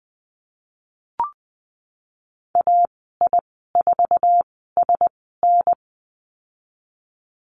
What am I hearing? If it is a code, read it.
Morse code, heard as a keyed tone, about 20 wpm, spaced as for 15 wpm, 700 Hz. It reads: AI4SN
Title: 048 Lesson 9g - Number 4 - Call Signs 20wpm